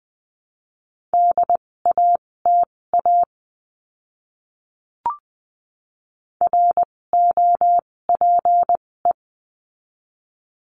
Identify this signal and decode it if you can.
Morse code, heard as a keyed tone, about 20 wpm, spaced as for 16 wpm, 700 Hz. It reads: DATA ROPE
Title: 256 Lesson 42a - Words 12-20wpm